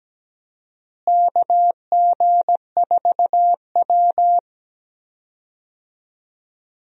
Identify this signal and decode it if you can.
Morse code, heard as a keyed tone, about 17 wpm, 700 Hz. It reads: KG4W
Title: USA Call Signs - 4 Characters - Repeated 3x - 17wpm